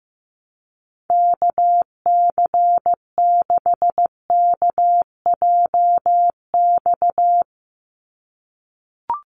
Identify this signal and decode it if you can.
Morse code, heard as a keyed tone, about 15 wpm, 700 Hz. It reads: KC6KJX
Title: USA Call Signs 15wpm